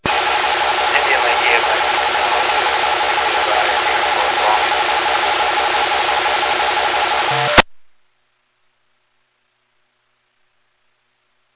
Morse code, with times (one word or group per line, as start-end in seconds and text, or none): none